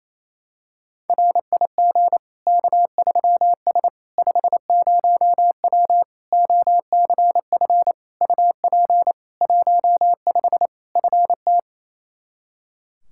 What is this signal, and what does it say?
Morse code, heard as a keyed tone, about 28 wpm, 700 Hz. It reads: RIG K3S 50W OCF UP 15 FT
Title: K1USN SST QSO Practice 28wpm